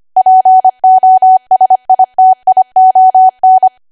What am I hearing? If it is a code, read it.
Morse code, heard as a keyed tone, about 25 wpm, 750 Hz. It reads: POSITION
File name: Archive / 25 WPM / 25WPM